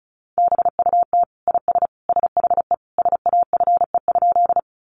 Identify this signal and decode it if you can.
Morse code, heard as a keyed tone, about 34 wpm, 700 Hz. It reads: BUT IS SHE SAFE?
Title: Sentences from Top 900 Words 35wpm